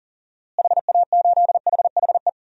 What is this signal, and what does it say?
Morse code, heard as a keyed tone, about 37 wpm, 700 Hz. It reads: HA8HHE
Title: DXCC-100 Call Signs - 6 Characters - Repeated 2x - 40wpm